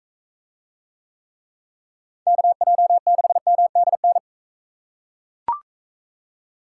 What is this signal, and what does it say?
Morse code, heard as a keyed tone, about 38 wpm, 700 Hz. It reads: KJ6MDN